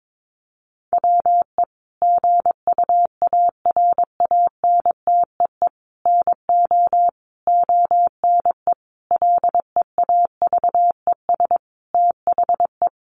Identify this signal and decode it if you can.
Morse code, heard as a keyed tone, about 22 wpm, 700 Hz. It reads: WE GUARANTEE NO ONE LEAVES THE
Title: A Sound Of Thunder 22wpm